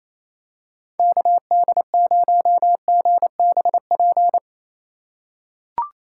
Morse code, heard as a keyed tone, about 28 wpm, 700 Hz. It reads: KD0GBP
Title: USA Call Signs - 6 Characters - 4x Character Spacing - 28wpm